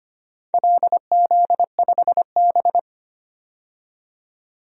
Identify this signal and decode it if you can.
Morse code, heard as a keyed tone, about 25 wpm, 700 Hz. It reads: LZ5B
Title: DXCC-100 Call Signs - 4 Characters - 25wpm